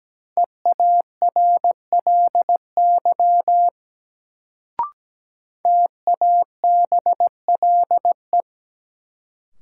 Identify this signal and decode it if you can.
Morse code, heard as a keyed tone, about 17 wpm, 700 Hz. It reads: EARLY TABLE
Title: Sentences from Top 400 Words - New Word Drill - 17wpm